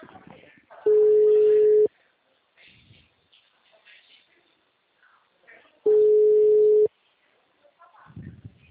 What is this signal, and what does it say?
Ringback tone (Europe, 425 Hz), with speech.